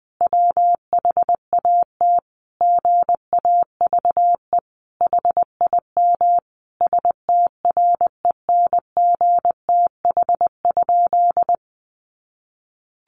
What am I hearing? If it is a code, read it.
Morse code, heard as a keyed tone, about 19 wpm, 700 Hz. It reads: WHAT GAVE HIM STRENGTH?
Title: Sentences from Top 1300 Words 20wpm